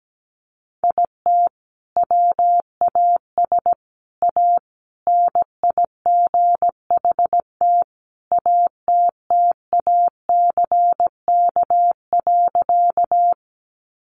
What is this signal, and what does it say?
Morse code, heard as a keyed tone, about 17 wpm, 700 Hz. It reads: IT WAS A NIGHT ATTACK.
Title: Sentences from Top 1500 Words 17wpm